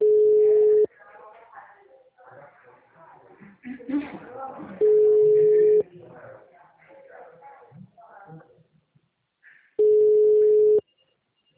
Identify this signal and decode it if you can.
Ringback tone (Europe, 425 Hz), with speech.